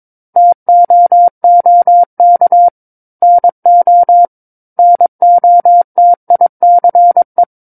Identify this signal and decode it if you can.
Morse code, heard as a keyed tone, about 22 wpm, 700 Hz. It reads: TOOK NO NOTICE
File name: 06-Chapter-06 22 WPM